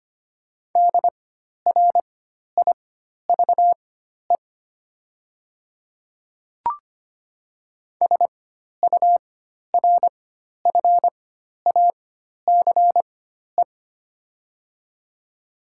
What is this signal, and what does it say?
Morse code, heard as a keyed tone, about 25 wpm, spaced as for 12 wpm, 700 Hz. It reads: DRIVE SURFACE
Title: 4x Character Spacing - Top 500 Words - 25wpm